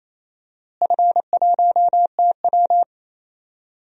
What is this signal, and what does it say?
Morse code, heard as a keyed tone, about 28 wpm, 700 Hz. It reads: F1TW
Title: DXCC-100 Call Signs - 4 Characters - Repeated 3x - 28wpm